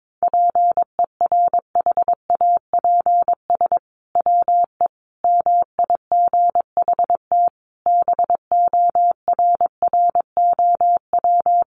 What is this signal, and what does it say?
Morse code, heard as a keyed tone, about 22 wpm, 700 Hz. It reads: PERHAPS WE MIGHT BORROW
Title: Sentences from Top 1500 Words 22wpm